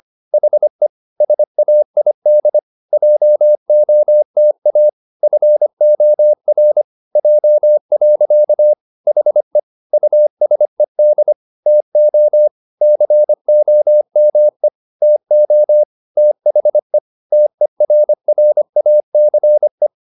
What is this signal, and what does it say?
Morse code, heard as a keyed tone, about 25 wpm, 600 Hz. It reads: HE SAID JOTA FOR J. HE USED TO COME TO THE TERRACE
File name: the old man and the sea part 1 25wpm